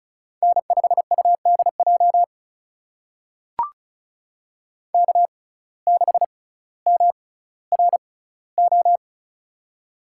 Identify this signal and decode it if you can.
Morse code, heard as a keyed tone, about 36 wpm, spaced as for 18 wpm, 700 Hz. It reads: N5UDJ K6MRO
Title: USA Call Signs - 6x Character Spacing - 35wpm